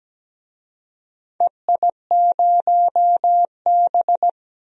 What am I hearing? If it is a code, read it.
Morse code, heard as a keyed tone, about 17 wpm, 700 Hz. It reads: EI0B
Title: DXCC-100 Call Signs - 4 Characters - 17wpm